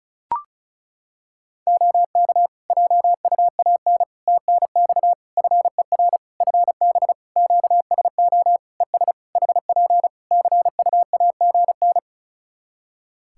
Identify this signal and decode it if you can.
Morse code, heard as a keyed tone, about 36 wpm, 700 Hz. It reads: OK JUAN TNX FER FB QSO ES HP CUAGN
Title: Practice QSO - Exchange 4 - Ending - 35wpm